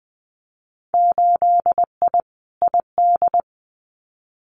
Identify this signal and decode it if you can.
Morse code, heard as a keyed tone, about 19 wpm, 700 Hz. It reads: 8I ID
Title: Winter Field Day Exchanges Encore 20wpm